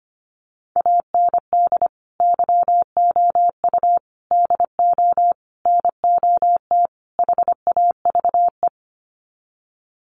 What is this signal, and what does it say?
Morse code, heard as a keyed tone, about 25 wpm, 690 Hz. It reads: AND YOU DO NOT HAVE
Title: Sets of 5 Words Encore 25wpm